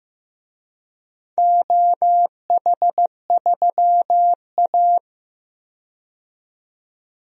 Morse code, heard as a keyed tone, about 15 wpm, 700 Hz. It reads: OH3A
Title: DXCC-100 Call Signs - 4 Characters - 15wpm